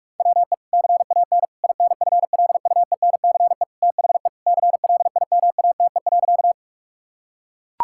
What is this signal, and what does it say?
Morse code, heard as a keyed tone, about 41 wpm, 700 Hz. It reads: WE CAN INFLUENCE THE CLIMATE.
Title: Sentences from Top 1600 Words 45wpm